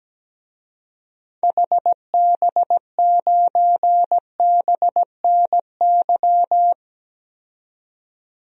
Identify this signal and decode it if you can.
Morse code, heard as a keyed tone, about 17 wpm, 700 Hz. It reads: HB9BNY
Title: DXCC-100 Call Signs - 6 Characters - 17wpm